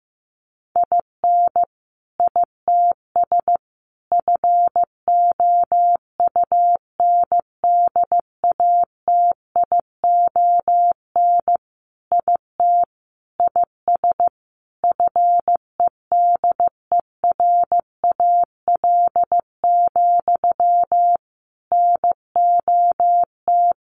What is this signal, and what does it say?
Morse code, heard as a keyed tone, about 14 wpm, 700 Hz. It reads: IN ITS FOUNDATION IT IS FEDERAL, NOT
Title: Sentences from Top 1200 Words 15wpm